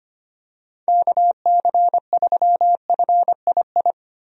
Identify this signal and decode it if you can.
Morse code, heard as a keyed tone, about 25 wpm, 700 Hz. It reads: KC3FII
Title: USA Call Signs - 6 Characters - Repeated 2x - 25wpm